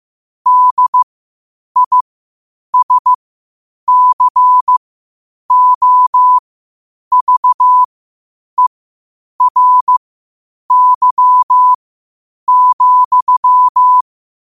Morse code, heard as a keyed tone, about 15 wpm, spaced as for 8 wpm, 1000 Hz. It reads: DISCOVERY,
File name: qotdfiles / 8wpm